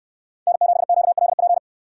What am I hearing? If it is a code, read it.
Morse code, heard as a keyed tone, about 67 wpm, 700 Hz. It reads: N7QLP